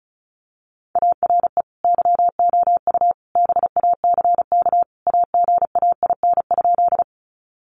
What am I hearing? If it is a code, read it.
Morse code, heard as a keyed tone, about 34 wpm, 700 Hz. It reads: ARE YOU BACK AGAIN?
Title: Sentences from Top 200 Words 35wpm